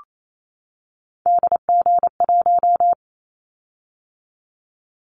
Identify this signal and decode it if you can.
Morse code, heard as a keyed tone, about 28 wpm, 700 Hz. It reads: DG1
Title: Callsign Prefixes for Top 25 DXCC - 28wpm